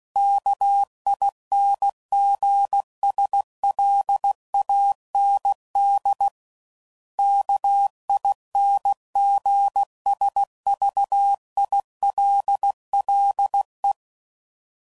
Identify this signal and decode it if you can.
Morse code, heard as a keyed tone, about 16 wpm, 800 Hz. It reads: KINGSLAND KINGSVILLE
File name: CW Code Groups @15WPM / fromSM's / Cities